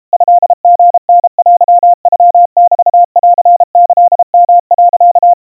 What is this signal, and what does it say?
Morse code (Japanese Wabun code), heard as a keyed tone, about 32 wpm, 700 Hz. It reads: トリタテノメンキヨ、
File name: kawa / Morse sample / kamyu 120 cpm 700 Hz Crispy